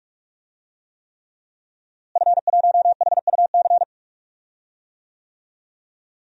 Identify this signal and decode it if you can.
Morse code, heard as a keyed tone, about 46 wpm, 700 Hz. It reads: F1HUC
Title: DXCC-100 Call Signs - 5 Characters - Repeated 3x - 45wpm